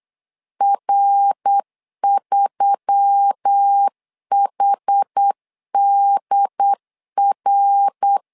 Morse code, heard as a keyed tone, about 8 wpm, 800 Hz. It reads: R3HDR
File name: LEZIONI / 4 Q1S3 8wpm 3037